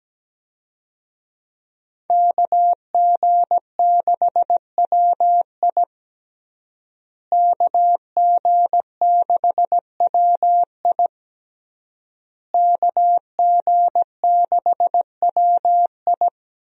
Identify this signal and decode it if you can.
Morse code, heard as a keyed tone, about 17 wpm, 700 Hz. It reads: KG6WI KG6WI KG6WI
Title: USA Call Signs - 5 Characters - Repeated 3x - 17wpm